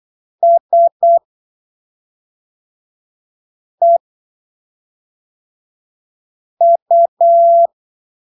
Morse code, heard as a keyed tone, about 8 wpm, spaced as for 3 wpm, 680 Hz. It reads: SEU
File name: arquivos / texto / texto 29 08ppm